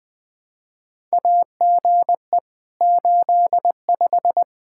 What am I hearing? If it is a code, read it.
Morse code, heard as a keyed tone, about 20 wpm, 700 Hz. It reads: AGE 85
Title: Practice QSO - Exchange 3 - More Info - 20wpm